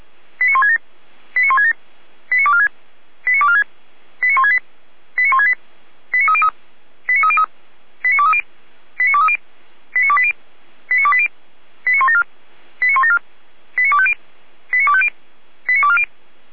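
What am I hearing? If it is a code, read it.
Five-tone selective call (ZVEI), heard as quick runs of five tones, tones of about 70 ms. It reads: 89167 89167 89246 89246 89168 89168 89303 89303 89230 89230 89280 89280 89174 89174 89250 89250 89240
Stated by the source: ZVEI five-tone selective call received by SDR.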